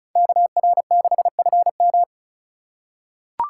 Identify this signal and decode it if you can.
Morse code, heard as a keyed tone, about 36 wpm, 700 Hz. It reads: KR6FM
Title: USA Call Signs - 5 Characters - 2x Character Spacing - 35wpm